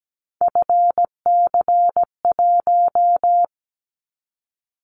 Morse code, heard as a keyed tone, about 17 wpm, 700 Hz. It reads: FC1
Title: Callsign Prefixes for Top 25 DXCC - 17wpm